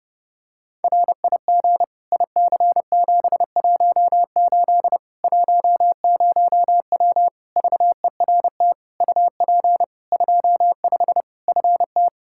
Morse code, heard as a keyed tone, about 30 wpm, 700 Hz. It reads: RIG IC718 10W VERT UP 25 FT
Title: Practice QSO - Exchange 2 - Info - 30wpm